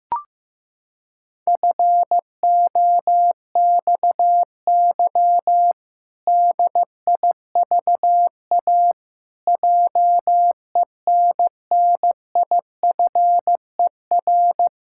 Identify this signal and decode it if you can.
Morse code, heard as a keyed tone, about 15 wpm, 700 Hz. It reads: FOXY DIVA JENNIFER
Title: Pangram Challenge v2 - 15wpm